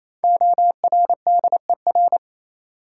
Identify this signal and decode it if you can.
Morse code, heard as a keyed tone, about 28 wpm, 700 Hz. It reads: ORDER
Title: Sets of 2 Words - New Word Drill - 28wpm